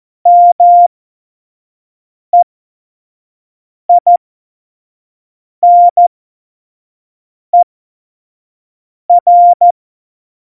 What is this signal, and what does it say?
Morse code, heard as a keyed tone, about 14 wpm, spaced as for 5 wpm, 700 Hz. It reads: MEINER
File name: Morsekurs / Witze / Witz 19. November 2019, 25 F70